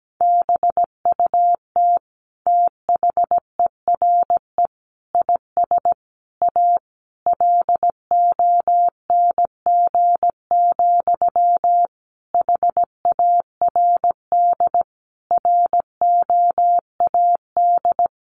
Morse code, heard as a keyed tone, about 17 wpm, 700 Hz. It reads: BUT THERE IS A LONG, HARD ROAD
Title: Sentences from Top 1100 Words 17wpm